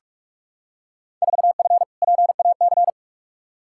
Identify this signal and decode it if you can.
Morse code, heard as a keyed tone, about 46 wpm, 700 Hz. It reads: 4F PAC
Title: Field Day Exchanges Encore - 45wpm